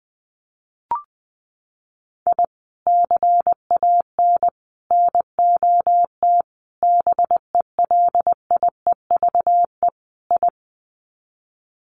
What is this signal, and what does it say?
Morse code, heard as a keyed tone, about 20 wpm, 700 Hz. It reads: I CAN NOT BELIEVE I
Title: Sets of 5 Words Encore2 20wpm